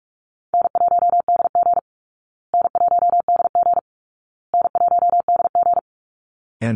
Morse code, heard as a keyed tone, about 44 wpm, 700 Hz. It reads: N1DG N1DG N1DG
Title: Easy Call Signs Repeated 3x - 45wpm